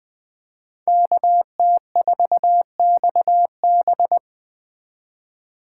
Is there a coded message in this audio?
Yes: Morse code from 0.9 to 4.2 s, heard as a keyed tone, about 20 wpm, 700 Hz. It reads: KT4XB